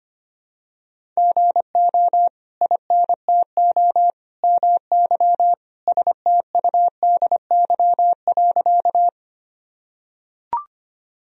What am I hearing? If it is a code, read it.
Morse code, heard as a keyed tone, about 25 wpm, 700 Hz. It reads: GO INTO MY STUDY.
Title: Sentences from Top 1000 Words - Review Part 1 - 25wpm